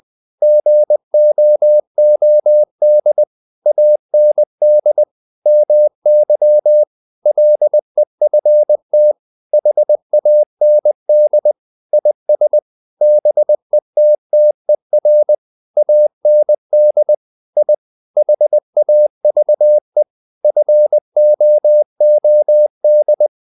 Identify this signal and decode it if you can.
Morse code, heard as a keyed tone, about 20 wpm, 600 Hz. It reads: GOOD AND MY LEFT HAND IS BETTER AND I HAVE FOOD
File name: the old man and the sea part 2 20wpm